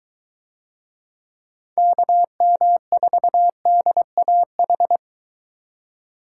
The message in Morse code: KM4DAH

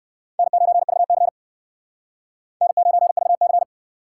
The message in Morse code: N9VC N9VC

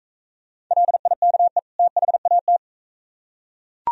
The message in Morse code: LIKE THAT

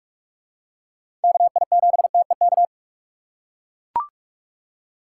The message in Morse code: KI7TEK

